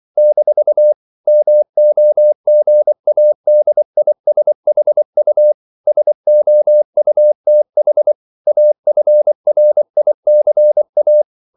Morse code, heard as a keyed tone, about 24 wpm, 600 Hz. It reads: - MOGADISHU SOUTH AFRICA
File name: CW / Practice / NationalCapitals24